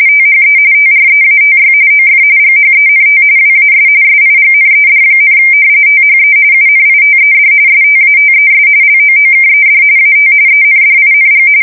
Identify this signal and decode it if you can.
RTTY, about 45.45 baud. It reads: THE TREATY OF GHENT WAS SIGNED, THE UNITED STATES LOOKED AT ENDING
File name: fsk170 / usnavy2